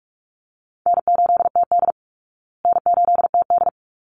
Morse code, heard as a keyed tone, about 44 wpm, 700 Hz. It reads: N8TD N8TD
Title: Call Signs Repeated 3x - 45wpm